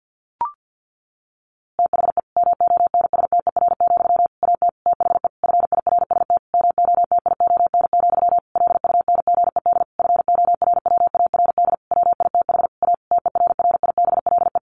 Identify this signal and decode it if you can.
Morse code, heard as a keyed tone, about 49 wpm, 700 Hz. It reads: THE MONSTER, AT THE FIRST MOTION, LUNGED FORWARD WITH A TERRIBLE
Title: A Sound Of Thunder 50wpm